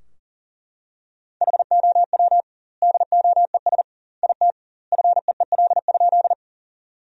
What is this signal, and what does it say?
Morse code, heard as a keyed tone, about 40 wpm, 700 Hz. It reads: HOW DOES IT FEEL?
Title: 2x Word-Spacing - Sentences from Top 400 Words - 40wpm